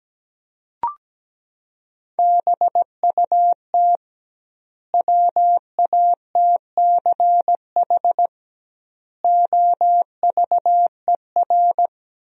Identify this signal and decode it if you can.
Morse code, heard as a keyed tone, about 17 wpm, 700 Hz. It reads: BUT WATCH OVER
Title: 2x Word-Spacing - Sentences from Top 300 Words - 17wpm